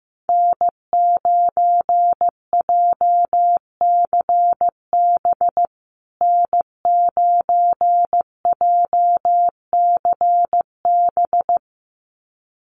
Morse code, heard as a keyed tone, about 15 wpm, 700 Hz. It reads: N9JCB N9JCB
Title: USA Call Signs Repeated 2x 15wpm